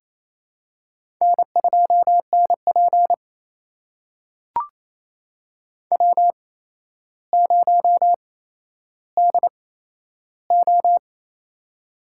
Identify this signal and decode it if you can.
Morse code, heard as a keyed tone, about 29 wpm, spaced as for 12 wpm, 700 Hz. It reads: N2NP W0DO
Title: USA Call Signs - 4 Characters - 8x Character Spacing - 28wpm